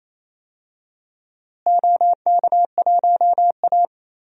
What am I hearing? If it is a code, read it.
Morse code, heard as a keyed tone, about 28 wpm, 700 Hz. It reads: OK1A
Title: DXCC-100 Call Signs - 4 Characters - Repeated 2x - 28wpm